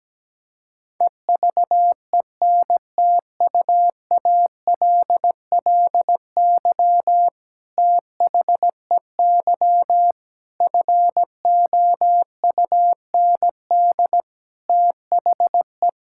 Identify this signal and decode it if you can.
Morse code, heard as a keyed tone, about 17 wpm, 700 Hz. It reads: EVENTUALLY THEY FOUND THE
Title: Sentences from Top 1000 Words - Review Part 6 - 17wpm